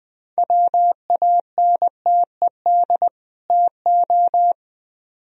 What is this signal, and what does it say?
Morse code, heard as a keyed tone, about 21 wpm, 700 Hz. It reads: WANTED TO